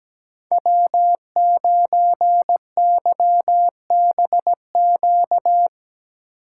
Morse code, heard as a keyed tone, about 17 wpm, 700 Hz. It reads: W9YBQ
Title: USA Call Signs - 5 Characters - 3x Character Spacing - 17wpm